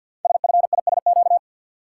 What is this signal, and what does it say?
Morse code, heard as a keyed tone, about 52 wpm, 700 Hz. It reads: SFISQ
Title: Code Groups - 5 Letter:Numbers - 50wpm